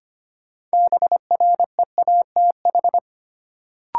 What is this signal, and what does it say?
Morse code, heard as a keyed tone, about 25 wpm, 700 Hz. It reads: BREATH
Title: Sets of 5 Words Encore - New Word Drill - 25wpm